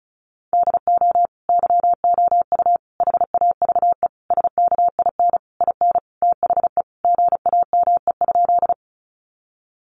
Morse code, heard as a keyed tone, about 34 wpm, 700 Hz. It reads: DO YOU HAVE SKIN IN THE GAME?
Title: Sentences from Top 1300 Words 35wpm